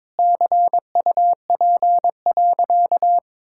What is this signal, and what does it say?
Morse code, heard as a keyed tone, about 22 wpm, 700 Hz. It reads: CUP.
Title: Sentences from Top 1600 Words 22wpm